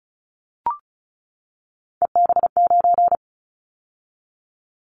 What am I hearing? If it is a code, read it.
Morse code, heard as a keyed tone, about 36 wpm, 700 Hz. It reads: EB9
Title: Callsign Prefixes for Top 100 DXCC - 35wpm